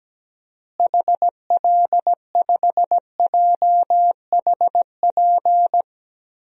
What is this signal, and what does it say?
Morse code, heard as a keyed tone, about 17 wpm, 700 Hz. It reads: HL5JHP
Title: DXCC-100 Call Signs - 6 Characters - Repeated 2x - 17wpm